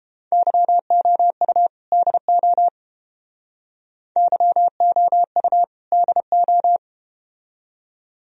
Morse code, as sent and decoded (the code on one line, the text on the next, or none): -.-- --- ..- / -.. --- / -.-- --- ..- / -.. ---
YOU DO YOU DO